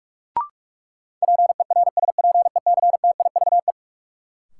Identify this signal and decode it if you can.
Morse code, heard as a keyed tone, about 42 wpm, 700 Hz. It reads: PERSPECTIVE
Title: Sentences from Top 1200 Words - New Word Drill - 45wpm